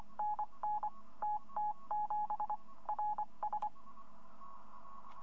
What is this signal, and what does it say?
Morse code, heard as a keyed tone, about 25 wpm, 840 Hz. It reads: N N T T 7 R S
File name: hfkup2018 / greske / yu2v